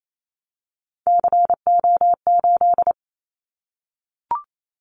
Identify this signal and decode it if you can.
Morse code, heard as a keyed tone, about 28 wpm, 700 Hz. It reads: CO8
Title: Callsign Prefixes for Top 100 DXCC - 28wpm